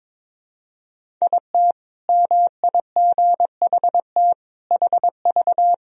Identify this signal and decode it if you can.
Morse code, heard as a keyed tone, about 22 wpm, 700 Hz. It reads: IT MIGHT HV
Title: Sentences from Top 500 Words and Abbreviations 22wpm